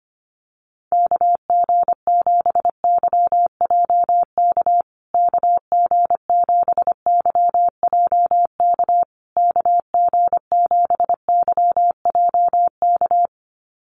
Morse code, heard as a keyed tone, about 25 wpm, 700 Hz. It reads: KG7YJK KG7YJK KG7YJK
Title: USA Call Signs Repeated 3x 25wpm